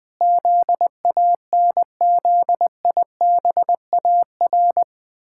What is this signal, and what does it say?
Morse code, heard as a keyed tone, about 20 wpm, 700 Hz. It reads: ZANZIBAR
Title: Pangram Challenge v2 - 20wpm